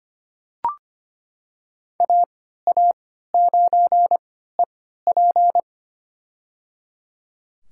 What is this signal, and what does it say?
Morse code, heard as a keyed tone, about 25 wpm, spaced as for 14 wpm, 700 Hz. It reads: AA9EP
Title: USA Call Signs - 5 Characters - 3x Character Spacing - 25wpm